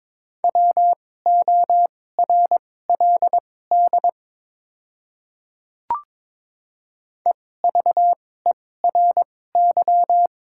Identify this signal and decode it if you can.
Morse code, heard as a keyed tone, about 22 wpm, spaced as for 16 wpm, 700 Hz. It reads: WORLD EVERY
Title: 2x Character Spacing - Top 200 Words - 22wpm